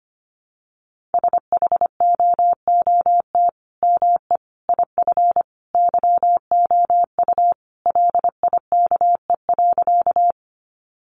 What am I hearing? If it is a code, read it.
Morse code, heard as a keyed tone, about 25 wpm, 700 Hz. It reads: SHOOT ME IF YOU LIKE.
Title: Sentences from Top 1200 Words 25wpm